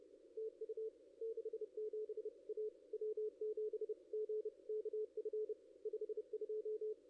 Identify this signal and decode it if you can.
Morse code, heard as a keyed tone, about 29 wpm, 440 Hz. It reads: TU 67 A W7GKF 52
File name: contests / ss-cw-k5tr-op-n5rz -1500